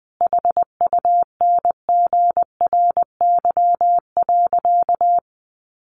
Morse code, heard as a keyed tone, about 20 wpm, 700 Hz. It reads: HUNGRY.
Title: Sentences from Top 1400 Words 20wpm